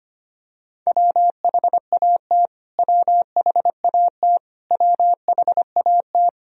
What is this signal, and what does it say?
Morse code, heard as a keyed tone, about 25 wpm, 700 Hz. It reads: WHAT WHAT WHAT
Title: Top 100 Words Repeated 25wpm